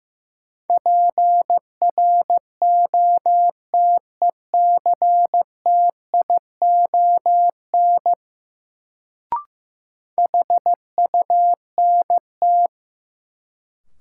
Morse code, heard as a keyed tone, about 16 wpm, 700 Hz. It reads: PROTECTION HUNT
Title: Sentences from Top 1600 Words - New Word Drill - 15wpm